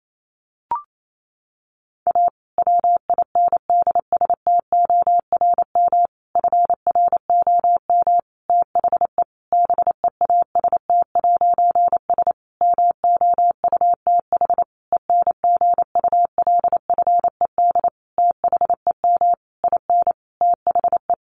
Morse code, heard as a keyed tone, about 28 wpm, 700 Hz. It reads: A WINDSTORM FROM THE BEAST'S MOUTH ENGULFED THEM IN THE
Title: A Sound Of Thunder 28wpm